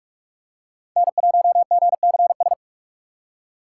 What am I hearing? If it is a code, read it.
Morse code, heard as a keyed tone, about 46 wpm, 700 Hz. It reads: N1GCS
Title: USA Call Signs - 8x Character Spacing - 45wpm